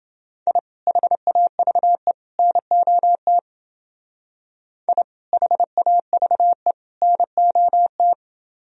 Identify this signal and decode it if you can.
Morse code, heard as a keyed tone, about 28 wpm, 700 Hz. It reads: I HAVE NOT I HAVE NOT